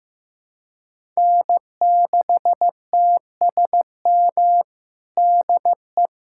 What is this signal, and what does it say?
Morse code, heard as a keyed tone, about 15 wpm, 700 Hz. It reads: N6TSM DE
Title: USA Callsign DE Callsign - 5 Characters - 15wpm